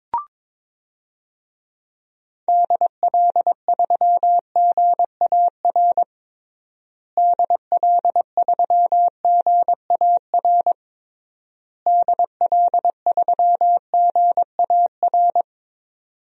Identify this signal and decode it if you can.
Morse code, heard as a keyed tone, about 22 wpm, 700 Hz. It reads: DL3GAR DL3GAR DL3GAR
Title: DXCC-100 Call Signs - 6 Characters - Repeated 3x - 22wpm